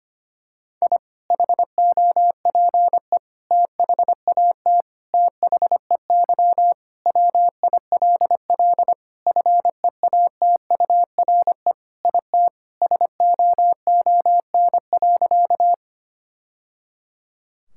Morse code, heard as a keyed tone, about 25 wpm, 700 Hz. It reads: I HOPE THAT THEY WILL FEATURE IT SOON.
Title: Sentences from Top 1000 Words - Review Part 5 - 25wpm